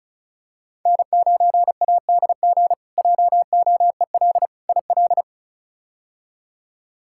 Morse code, heard as a keyed tone, about 36 wpm, 700 Hz. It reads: N9ADG JOEL IL
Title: K1USN SST QSO Practice - 35wpm